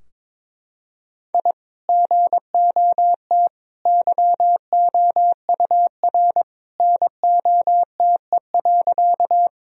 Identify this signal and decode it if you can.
Morse code, heard as a keyed tone, about 22 wpm, 700 Hz. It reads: I GOT YOUR NOTE.
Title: Sentences from Top 1000 Words - Review Part 2 - 22wpm